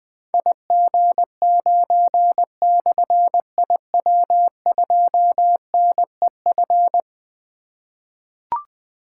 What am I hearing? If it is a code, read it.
Morse code, heard as a keyed tone, about 20 wpm, 700 Hz. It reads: IG9/IW2NEF
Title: Difficult Call Signs 20wpm